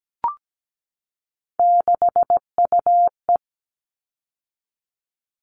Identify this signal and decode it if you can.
Morse code, heard as a keyed tone, about 17 wpm, 700 Hz. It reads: 6UE
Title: All Cut Numbers - 3-Digit 17wpm